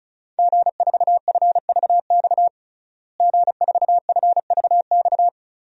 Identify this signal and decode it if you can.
Morse code, heard as a keyed tone, about 36 wpm, 700 Hz. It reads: G4FVX G4FVX
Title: DXCC-100 Call Signs - 5 Characters - Repeated 2x - 35wpm